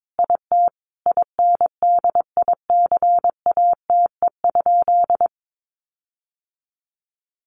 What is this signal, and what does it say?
Morse code, heard as a keyed tone, about 22 wpm, 700 Hz. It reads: IT INDICATE?
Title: Sentences from Top 1300 Words 22wpm